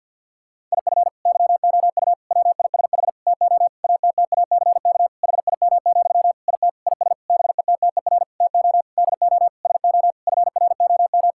Morse code, heard as a keyed tone, about 50 wpm, 700 Hz. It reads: IF YOU WISH TO ATTACK HIM, IT IS BETTER TO DO SO FROM
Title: Sentences from Top 1600 Words 50wpm